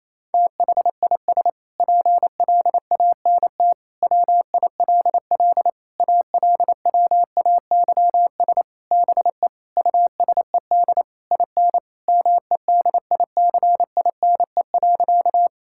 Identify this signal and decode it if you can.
Morse code, heard as a keyed tone, about 28 wpm, 700 Hz. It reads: THIS PLANT WILL ALWAYS BE USED IN MEDICINE.
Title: Sentences from Top 1000 Words - Review Part 6 - 28wpm